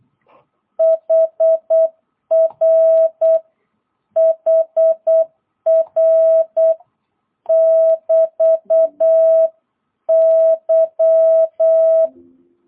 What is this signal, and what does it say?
Morse code, heard as a keyed tone, about 8 wpm, 660 Hz. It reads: HR HR <BT> Y
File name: jh8cbh / 2022cw